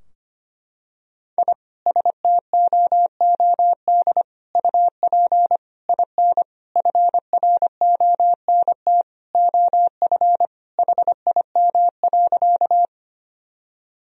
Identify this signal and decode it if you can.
Morse code, heard as a keyed tone, about 25 wpm, 700 Hz. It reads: I STOOD UP IN FRONT OF HIM.
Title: Sentences from Top 1000 Words - Review Part 2 - 25wpm